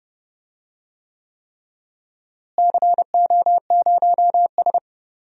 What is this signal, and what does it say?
Morse code, heard as a keyed tone, about 32 wpm, 700 Hz. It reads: CO0S
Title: DXCC-100 Call Signs - 4 Characters - Repeated 3x - 30wpm